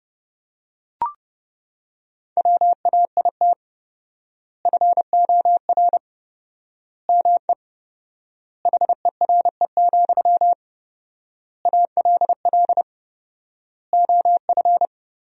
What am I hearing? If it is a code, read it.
Morse code, heard as a keyed tone, about 32 wpm, 700 Hz. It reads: WAIT FOR ME HERE, ALL OF
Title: 4x Word-Spacing - Sentences from Top 500 Words - 30wpm